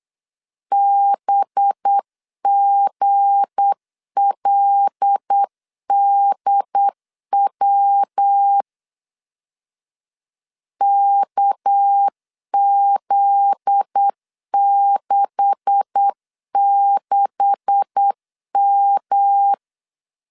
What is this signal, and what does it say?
Morse code, heard as a keyed tone, about 8 wpm, 800 Hz. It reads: BGLDW KZ66M
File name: LEZIONI / 9 286 8wpm 3425